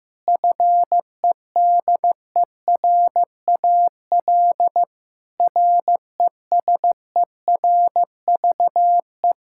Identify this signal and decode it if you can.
Morse code, heard as a keyed tone, about 15 wpm, 700 Hz. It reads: FEDERAL RESERVE
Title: Sentences from Top 1000 Words - Review Part 6 - 15wpm